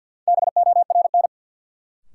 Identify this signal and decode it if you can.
Morse code, heard as a keyed tone, about 53 wpm, 700 Hz. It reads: BORN
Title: Sentences from Top 1000 Words - New Word Drill - 50wpm